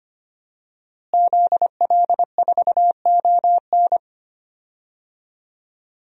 Morse code, heard as a keyed tone, about 25 wpm, 700 Hz. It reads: ZL4ON
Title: DXCC-100 Call Signs - 5 Characters - Repeated 3x - 25wpm